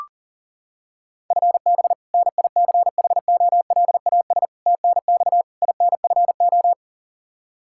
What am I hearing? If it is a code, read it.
Morse code, heard as a keyed tone, about 40 wpm, 700 Hz. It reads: FB NICHOLAS TNX INFO
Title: K1USN SST QSO Practice 40wpm